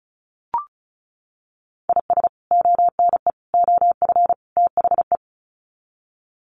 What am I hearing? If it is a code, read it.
Morse code, heard as a keyed tone, about 34 wpm, 700 Hz. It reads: IS ONE OF THE
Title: Sets of 4 Words 35wpm